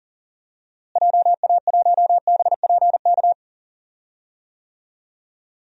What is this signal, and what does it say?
Morse code, heard as a keyed tone, about 40 wpm, 700 Hz. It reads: JA1BPK
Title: DXCC-100 Call Signs - 6 Characters - Repeated 3x - 40wpm